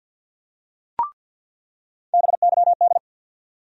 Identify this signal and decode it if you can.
Morse code, heard as a keyed tone, about 48 wpm, 700 Hz. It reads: BYD
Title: USA Call Sign Postfixes - 3 Letters 50wpm